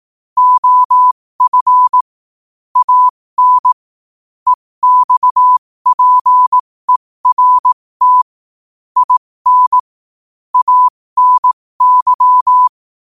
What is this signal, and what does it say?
Morse code, heard as a keyed tone, about 18 wpm, spaced as for 15 wpm, 1000 Hz. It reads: OF AN EXPERT IN ANY
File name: qotdfiles / 15wpm